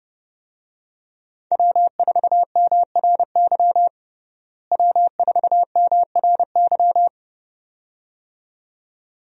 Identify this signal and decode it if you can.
Morse code, heard as a keyed tone, about 31 wpm, 700 Hz. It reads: W4MRY W4MRY
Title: USA Call Signs - 5 Characters - Repeated 2x - 30wpm